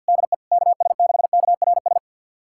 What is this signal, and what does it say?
Morse code, heard as a keyed tone, about 48 wpm, 690 Hz. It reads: DE KS6KRS
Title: USA Callsign DE Callsign 50wpm